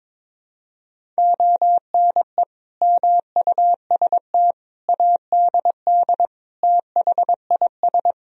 Morse code, heard as a keyed tone, about 22 wpm, 700 Hz. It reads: ONE MUST ADD THIS
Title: Sentences from Top 1000 Words - Review Part 1 - 22wpm